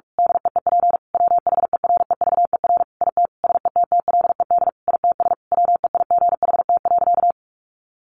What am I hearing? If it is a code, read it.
Morse code, heard as a keyed tone, about 44 wpm, 700 Hz. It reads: DEEP WHEREVER IT SETTLED ITS WEIGHT.
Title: A Sound Of Thunder 45wpm